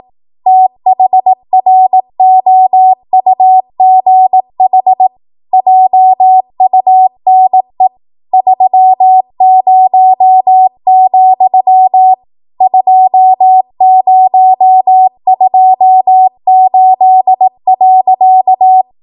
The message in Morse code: THROUGH JUNE 30, 2028.

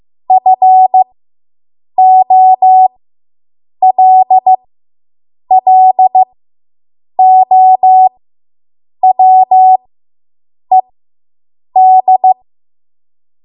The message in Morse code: FOLLOWED

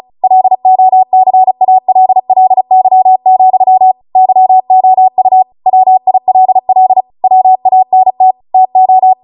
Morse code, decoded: LOCALLY, YOU WILL WANT TO

